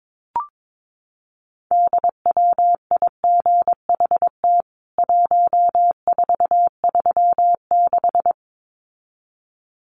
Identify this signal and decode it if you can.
Morse code, heard as a keyed tone, about 22 wpm, 700 Hz. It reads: DWIGHT 1436
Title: CWT Exchange 22wpm